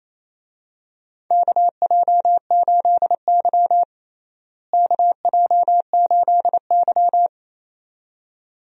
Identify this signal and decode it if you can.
Morse code, heard as a keyed tone, about 28 wpm, 700 Hz. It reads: KJ8Y KJ8Y
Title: USA Call Signs - 4 Characters - Repeated 2x - 28wpm